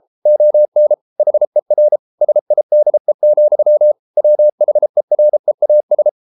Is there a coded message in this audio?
Yes: Morse code from 0.3 to 6.1 s, heard as a keyed tone, about 33 wpm, 600 Hz. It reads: ON HER SIDE, WHEREAS